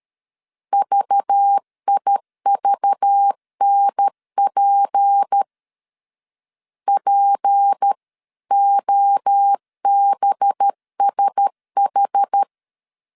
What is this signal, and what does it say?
Morse code, heard as a keyed tone, about 13 wpm, 800 Hz. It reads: VIVNP POBSH
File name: LEZIONI / 6 PKX0 / 6 P 12wpm 2033